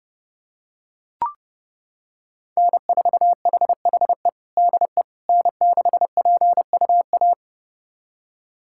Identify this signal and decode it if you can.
Morse code, heard as a keyed tone, about 30 wpm, 700 Hz. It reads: N4HHE DE N6PUA
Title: USA Callsign DE Callsign - 5 Characters - 30wpm